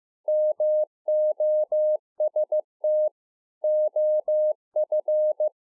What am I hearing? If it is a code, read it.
Morse code, heard as a keyed tone, about 15 wpm, 610 Hz. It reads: MOST OF